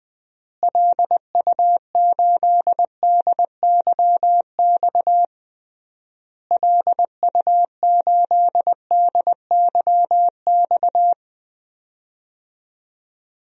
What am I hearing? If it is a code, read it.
Morse code, heard as a keyed tone, about 20 wpm, 700 Hz. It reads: LU8DYX LU8DYX
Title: DXCC-100 Call Signs - 6 Characters - Repeated 3x - 20wpm